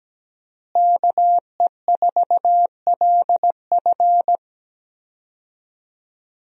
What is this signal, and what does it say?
Morse code, heard as a keyed tone, about 17 wpm, 700 Hz. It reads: KE4LF
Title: USA Call Signs - 5 Characters - Repeated 2x - 17wpm